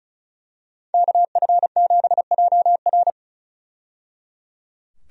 Morse code, heard as a keyed tone, about 36 wpm, 700 Hz. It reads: KF7JR
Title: USA Call Signs - 5 Characters - 35wpm